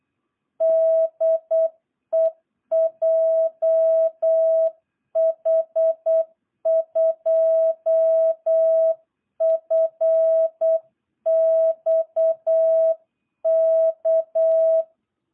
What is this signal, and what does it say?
Morse code, heard as a keyed tone, about 8 wpm, 660 Hz. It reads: DE JH2FXK
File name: jh8cbh / 2022cw